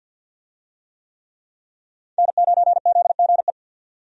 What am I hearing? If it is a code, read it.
Morse code, heard as a keyed tone, about 51 wpm, 700 Hz. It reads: N9ZGE
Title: USA Call Signs - 5 Characters - 50wpm